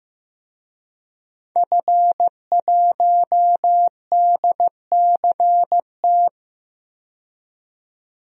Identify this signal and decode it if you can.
Morse code, heard as a keyed tone, about 15 wpm, 700 Hz. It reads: F1DCT
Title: DXCC-100 Call Signs - 5 Characters - 15wpm